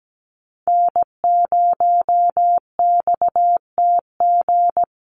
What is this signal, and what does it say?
Morse code, heard as a keyed tone, about 17 wpm, 700 Hz. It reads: N0XTG
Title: USA Call Signs Repeated 3x 17wpm